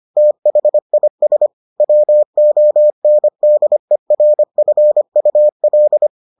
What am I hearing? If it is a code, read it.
Morse code, heard as a keyed tone, about 25 wpm, 600 Hz. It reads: THIS WONDERFUL